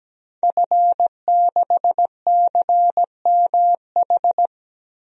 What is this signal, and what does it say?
Morse code, heard as a keyed tone, about 17 wpm, 700 Hz. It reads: F6CMH
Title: DXCC-100 Call Signs - 5 Characters - Repeated 3x - 17wpm